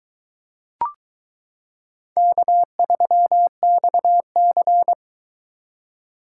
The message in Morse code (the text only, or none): K3XC